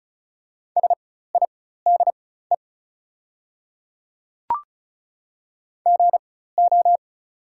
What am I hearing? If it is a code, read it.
Morse code, heard as a keyed tone, about 36 wpm, spaced as for 16 wpm, 700 Hz. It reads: SIDE GO
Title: 4x Character Spacing - Top 100 Words - 35wpm